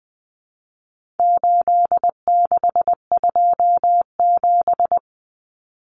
Morse code, heard as a keyed tone, about 20 wpm, 700 Hz. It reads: 8627
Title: Numbers - 4-Digit 20wpm